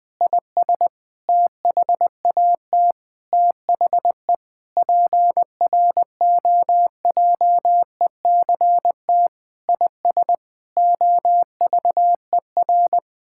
Morse code, heard as a keyed tone, about 21 wpm, 700 Hz. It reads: IS THAT THE PROJECT IS OVER
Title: Sentences with 8-Word Phrases - Spoken Beginning - 20wpm